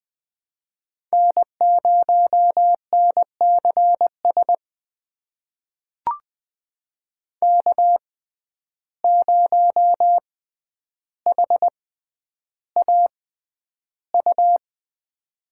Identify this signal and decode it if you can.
Morse code, heard as a keyed tone, about 20 wpm, spaced as for 10 wpm, 700 Hz. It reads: N0NCS K0HAU
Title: USA Call Signs - 5 Characters - 6x Character Spacing - 20wpm